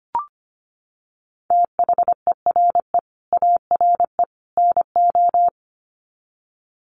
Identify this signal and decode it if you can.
Morse code, heard as a keyed tone, about 25 wpm, 700 Hz. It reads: THERE ARE NO
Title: Sets of 3 Words 25wpm